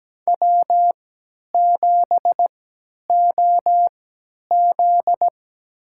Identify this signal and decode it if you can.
Morse code, heard as a keyed tone, about 17 wpm, spaced as for 10 wpm, 700 Hz. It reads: W7OZ
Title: USA Call Signs - 4 Characters - 3x Character Spacing - 17wpm